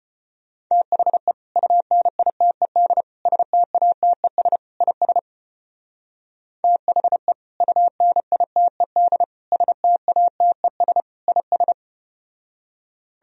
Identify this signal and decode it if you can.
Morse code, heard as a keyed tone, about 33 wpm, 700 Hz. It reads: THE UNITED STATES IS THE UNITED STATES IS